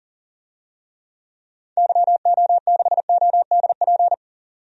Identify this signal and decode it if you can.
Morse code, heard as a keyed tone, about 40 wpm, 700 Hz. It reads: YO6ODP
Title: DXCC-100 Call Signs - 6 Characters - Repeated 2x - 40wpm